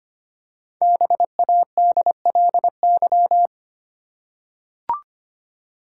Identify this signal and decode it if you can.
Morse code, heard as a keyed tone, about 25 wpm, 700 Hz. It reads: BADLY
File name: 5 Letter Words - First 3 Letters Given - 25wpm